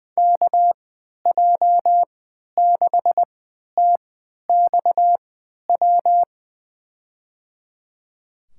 Morse code, heard as a keyed tone, about 21 wpm, spaced as for 11 wpm, 700 Hz. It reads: KJ6TXW
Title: USA Call Signs - 3x Character Spacing - 20wpm